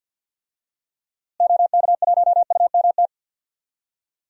Morse code, heard as a keyed tone, about 46 wpm, 700 Hz. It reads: OK1UMT
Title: DXCC-100 Call Signs - 6 Characters - Repeated 3x - 50wpm